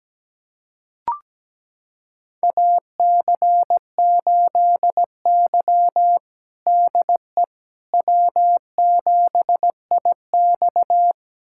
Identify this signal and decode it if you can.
Morse code, heard as a keyed tone, about 17 wpm, 700 Hz. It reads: AC8Y DE W7IX
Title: USA Callsign DE Callsign - 4 Characters - 17wpm